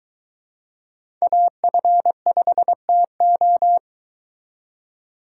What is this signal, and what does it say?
Morse code, heard as a keyed tone, about 23 wpm, 700 Hz. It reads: AF5TO